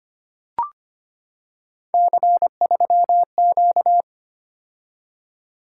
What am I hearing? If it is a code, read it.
Morse code, heard as a keyed tone, about 25 wpm, 700 Hz. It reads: C3Q
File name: Three Letter-Number Combinations - 25wpm